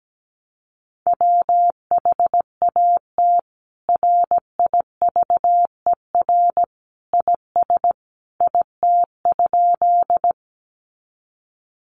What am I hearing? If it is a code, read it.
Morse code, heard as a keyed tone, about 17 wpm, 700 Hz. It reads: WHAT RIVER IS IT?
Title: Sentences from Top 300 Words 17wpm